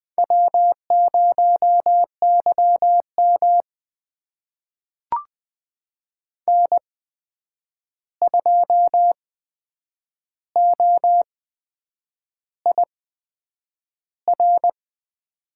Morse code, heard as a keyed tone, about 20 wpm, spaced as for 8 wpm, 700 Hz. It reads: W0YM N2OIR
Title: USA Call Signs - 8x Character Spacing - 20wpm